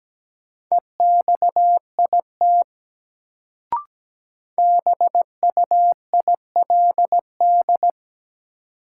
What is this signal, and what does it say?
Morse code, heard as a keyed tone, about 17 wpm, 700 Hz. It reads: EXIT BUILD
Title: Sentences from Top 1100 Words - New Word Drill Alt - 17wpm